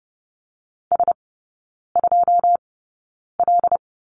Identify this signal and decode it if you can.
Morse code, heard as a keyed tone, about 30 wpm, spaced as for 9 wpm, 700 Hz. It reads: S2L
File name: Lesson 18e - Letter m - Three Characters 30wpm